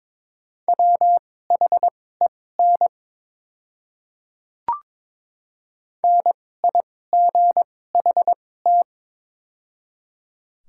Morse code, heard as a keyed tone, about 22 wpm, spaced as for 16 wpm, 700 Hz. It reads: WHEN NIGHT
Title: 2x Character Spacing - Top 300 Words - 22wpm